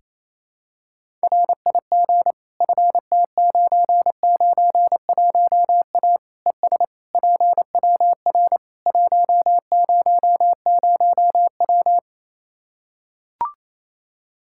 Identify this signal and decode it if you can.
Morse code, heard as a keyed tone, about 28 wpm, 700 Hz. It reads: RIG FT991A ES PWR 100W
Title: Practice QSO - Exchange 2 - Info - 28wpm